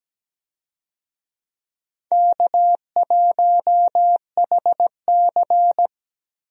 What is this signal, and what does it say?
Morse code, heard as a keyed tone, about 17 wpm, 700 Hz. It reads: K1HC
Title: USA Call Signs - 4 Characters - Repeated 2x - 17wpm